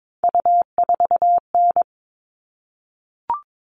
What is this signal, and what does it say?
Morse code, heard as a keyed tone, about 22 wpm, 700 Hz. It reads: U4N
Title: All Cut Numbers - 3-Digit 22wpm